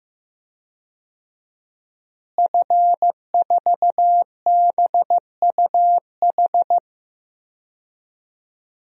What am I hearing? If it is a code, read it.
Morse code, heard as a keyed tone, about 15 wpm, 700 Hz. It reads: F4BUH
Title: DXCC-100 Call Signs - 5 Characters - Repeated 3x - 15wpm